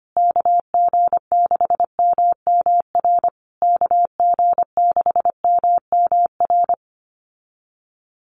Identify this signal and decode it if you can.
Morse code, heard as a keyed tone, about 25 wpm, 700 Hz. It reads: KG6MMR KG6MMR
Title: USA Call Signs Repeated 3x 25wpm